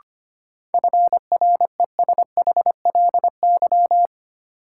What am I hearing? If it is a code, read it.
Morse code, heard as a keyed tone, about 25 wpm, 700 Hz. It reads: FRESHLY
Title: Sets of 5 Words Encore - New Word Drill - 25wpm